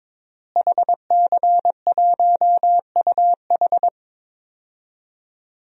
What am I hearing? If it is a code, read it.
Morse code, heard as a keyed tone, about 22 wpm, 700 Hz. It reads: HC1UH
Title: DXCC-100 Call Signs - 5 Characters - Repeated 3x - 22wpm